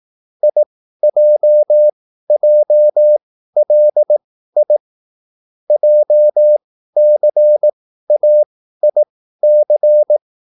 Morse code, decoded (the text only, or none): IJJLI JCAIC